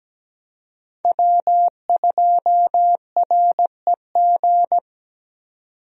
Morse code, heard as a keyed tone, about 17 wpm, 700 Hz. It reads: W2REG
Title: USA Call Signs - 5 Characters - 17wpm